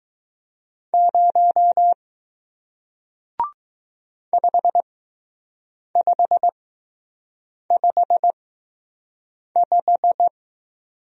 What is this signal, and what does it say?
Morse code, heard as a keyed tone, about 20 wpm, spaced as for 6 wpm, 700 Hz. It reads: 0 5555